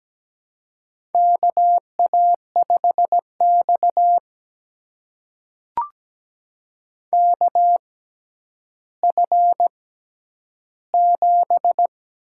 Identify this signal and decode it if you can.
Morse code, heard as a keyed tone, about 17 wpm, spaced as for 10 wpm, 700 Hz. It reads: KA5X KF7
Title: USA Call Signs - 6x Character Spacing - 17wpm